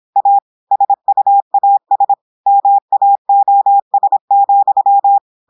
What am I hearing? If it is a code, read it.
Morse code, heard as a keyed tone, about 26 wpm, 800 Hz. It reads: A SUAS MAOS,
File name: audios / portugais / portugais texte en clair 1 vitesse 26 mots minutes